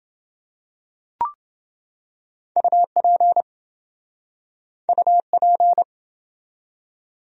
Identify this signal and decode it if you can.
Morse code, heard as a keyed tone, about 29 wpm, 700 Hz. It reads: UP UP